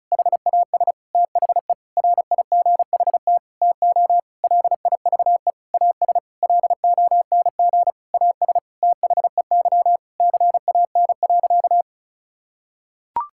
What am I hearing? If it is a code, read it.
Morse code, heard as a keyed tone, about 36 wpm, 700 Hz. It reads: HAS THE RIGHT TO LIVE AS LONG AS THEY CAN.
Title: Sentences from Top 1000 Words - Review Part 5 - 35wpm